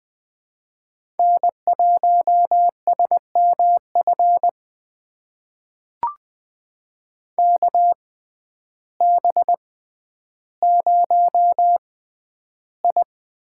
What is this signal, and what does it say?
Morse code, heard as a keyed tone, about 20 wpm, spaced as for 11 wpm, 700 Hz. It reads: N1SMF KB0I
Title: USA Call Signs - 6x Character Spacing - 20wpm